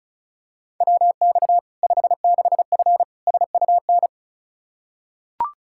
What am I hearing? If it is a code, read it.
Morse code, heard as a keyed tone, about 36 wpm, 700 Hz. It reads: WX 56F SUN
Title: Practice QSO - Exchange 2 - Info - 35wpm